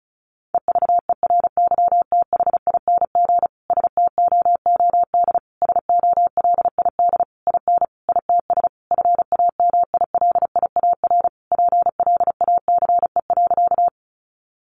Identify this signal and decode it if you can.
Morse code, heard as a keyed tone, about 34 wpm, 700 Hz. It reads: EVERYTHING STOOD SOLID IN ITS FAMILIAR PLACE.
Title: Sentences from Top 1500 Words 35wpm